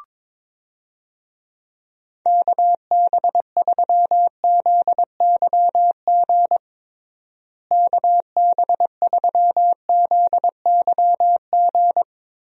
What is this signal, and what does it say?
Morse code, heard as a keyed tone, about 22 wpm, 700 Hz. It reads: KB3ZYG KB3ZYG
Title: USA Call Signs - 6 Characters - Repeated 3x - 22wpm